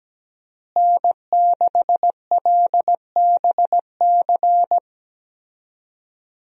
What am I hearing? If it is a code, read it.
Morse code, heard as a keyed tone, about 17 wpm, 700 Hz. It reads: N6LBC
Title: USA Call Signs - 5 Characters - Repeated 3x - 17wpm